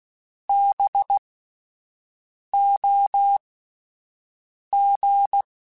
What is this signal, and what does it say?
Morse code, heard as a keyed tone, about 16 wpm, spaced as for 5 wpm, 800 Hz. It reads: BOG